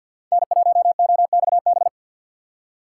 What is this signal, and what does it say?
Morse code, heard as a keyed tone, about 49 wpm, 700 Hz. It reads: N1OXB
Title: USA Call Signs - 5 Characters - 4x Character Spacing - 50wpm